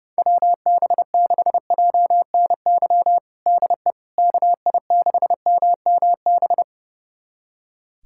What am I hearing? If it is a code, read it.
Morse code, heard as a keyed tone, about 30 wpm, 700 Hz. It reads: WB6JNY DE KI6MMB
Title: USA Callsign DE Callsign 30wpm